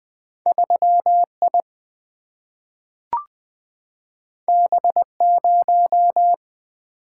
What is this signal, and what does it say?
Morse code, heard as a keyed tone, about 20 wpm, 700 Hz. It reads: 3I B0
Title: Single-Letter Single-Number 20wpm